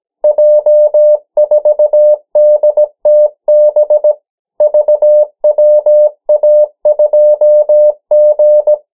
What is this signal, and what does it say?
Morse code, heard as a keyed tone, about 17 wpm, 600 Hz. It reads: J4DTB VWA2G
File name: morse mp3 / 14 urok 13